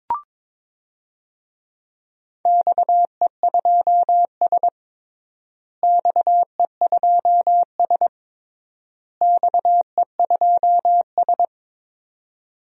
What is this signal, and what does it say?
Morse code, heard as a keyed tone, about 22 wpm, 700 Hz. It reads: XE2S XE2S XE2S
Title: DXCC-100 Call Signs - 4 Characters - Repeated 3x - 22wpm